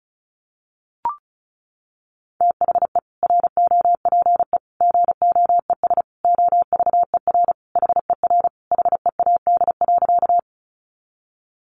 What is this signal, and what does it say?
Morse code, heard as a keyed tone, about 34 wpm, 700 Hz. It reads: THE ROPE GOES OVER HER HEAD.
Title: Easy Phrases 35wpm